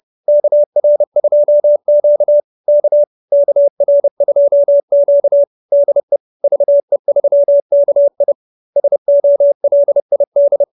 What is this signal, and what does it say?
Morse code, heard as a keyed tone, about 30 wpm, 570 Hz. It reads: KR2Q K KR2Q DE VE3KI SOLID